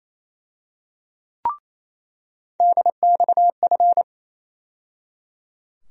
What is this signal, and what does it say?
Morse code, heard as a keyed tone, about 27 wpm, 700 Hz. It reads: DXF
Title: USA Call Sign Postfixes - 3 Letters 28wpm